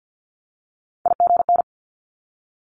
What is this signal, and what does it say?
Morse code, heard as a keyed tone, about 50 wpm, 700 Hz. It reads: IGN
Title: Multi-letter Phonemes 50wpm